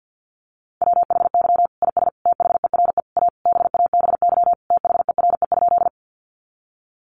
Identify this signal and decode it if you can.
Morse code, heard as a keyed tone, about 50 wpm, 700 Hz. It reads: WHY IS THERE A BABY THERE?
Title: Sentences from Top 1300 Words 50wpm